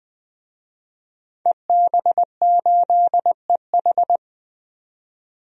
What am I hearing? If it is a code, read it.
Morse code, heard as a keyed tone, about 20 wpm, 700 Hz. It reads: EB8EH
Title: DXCC-100 Call Signs - 5 Characters - 20wpm